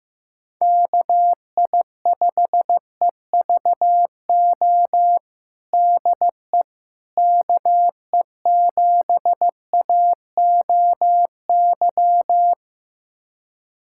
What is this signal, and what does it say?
Morse code, heard as a keyed tone, about 15 wpm, 700 Hz. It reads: KI5EVO DE KE7AOY
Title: USA Callsign DE Callsign 15wpm